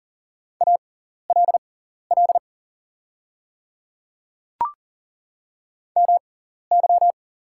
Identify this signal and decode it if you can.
Morse code, heard as a keyed tone, about 40 wpm, spaced as for 14 wpm, 690 Hz. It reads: ALL MY
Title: 6x Character Spacing - Top 100 Words - 40wpm